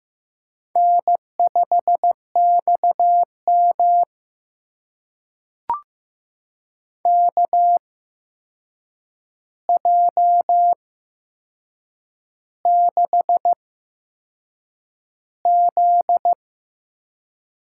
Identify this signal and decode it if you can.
Morse code, heard as a keyed tone, about 15 wpm, spaced as for 6 wpm, 700 Hz. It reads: N5XM KJ6Z
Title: USA Call Signs - 8x Character Spacing - 15wpm